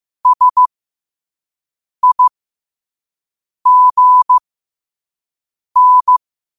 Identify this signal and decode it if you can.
Morse code, heard as a keyed tone, about 15 wpm, spaced as for 5 wpm, 1000 Hz. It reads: SIGN
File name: qotdfiles / 5wpm